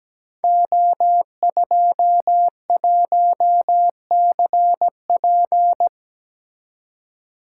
Morse code, heard as a keyed tone, about 17 wpm, 700 Hz. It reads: O21CP
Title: Code Groups - 5 Letter:Numbers - 17wpm